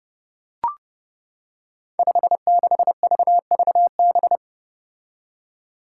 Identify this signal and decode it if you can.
Morse code, heard as a keyed tone, about 30 wpm, 700 Hz. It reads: 56VVB
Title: Five Challenging Letter-Number Combinations 30wpm